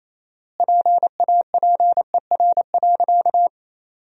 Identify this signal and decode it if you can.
Morse code, heard as a keyed tone, about 28 wpm, 700 Hz. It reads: PAPER.
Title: 4x Word-Spacing - Sentences from Top 500 Words - 28wpm